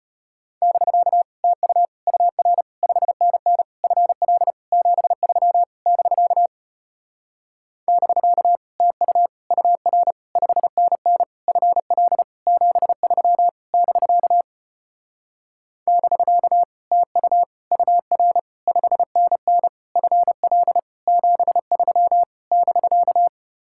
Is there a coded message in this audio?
Yes: Morse code from 0.6 to 23.3 s, heard as a keyed tone, about 33 wpm, 700 Hz. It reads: <BK> TU UR 5NN FL 73 <BK> <BK> TU UR 5NN FL 73 <BK> <BK> TU UR 5NN FL 73 <BK>